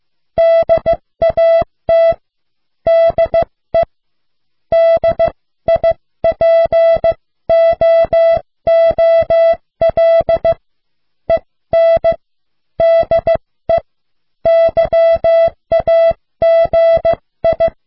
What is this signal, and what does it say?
Morse code, heard as a keyed tone, about 15 wpm, spaced as for 13 wpm, 660 Hz. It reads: DAT DE DIPOOL EN DE YAGI